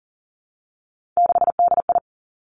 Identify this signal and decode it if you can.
Morse code, heard as a keyed tone, about 40 wpm, 700 Hz. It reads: 6DI
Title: Three Challenging Letter-Number Combinations 40wpm